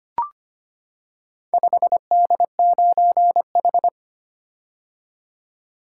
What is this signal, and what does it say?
Morse code, heard as a keyed tone, about 25 wpm, 700 Hz. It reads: 5D9H
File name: practice / 25wpm / Four Letter-Number Combinations - 25wpm